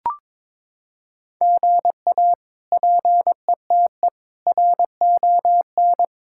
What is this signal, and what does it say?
Morse code, heard as a keyed tone, about 21 wpm, 700 Hz. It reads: GA PETE RON
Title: K1USN SST QSO Practice - 22wpm